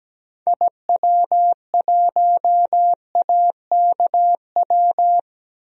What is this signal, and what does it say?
Morse code, heard as a keyed tone, about 17 wpm, 700 Hz. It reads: IW1AKW
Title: DXCC-100 Call Signs - 6 Characters - Repeated 3x - 17wpm